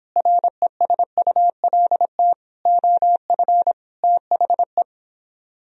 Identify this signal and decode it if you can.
Morse code, heard as a keyed tone, about 31 wpm, 700 Hz. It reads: RESULT OF THE